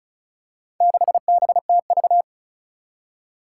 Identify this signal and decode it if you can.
Morse code, heard as a keyed tone, about 36 wpm, 700 Hz. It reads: 6BTV
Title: Four Challenging Letter-Number Combinations 35wpm